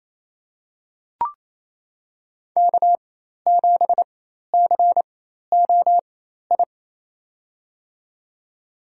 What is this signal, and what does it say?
Morse code, heard as a keyed tone, about 28 wpm, spaced as for 13 wpm, 690 Hz. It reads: K7COI
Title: USA Call Signs - 5 Characters - 4x Character Spacing - 28wpm